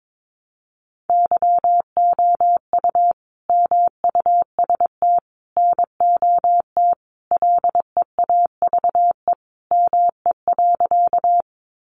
Morse code, heard as a keyed tone, about 22 wpm, 700 Hz. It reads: YOU MUST NOT LEAVE ME.
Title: Sentences from Top 600 Words 22wpm